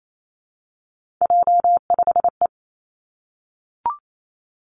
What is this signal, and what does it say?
Morse code, heard as a keyed tone, about 28 wpm, 700 Hz. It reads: J5E
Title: Three Challenging Letter-Number Combinations 28wpm